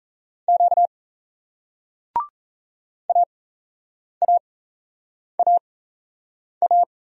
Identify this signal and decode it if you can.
Morse code, heard as a keyed tone, about 36 wpm, spaced as for 8 wpm, 700 Hz. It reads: Q AAAA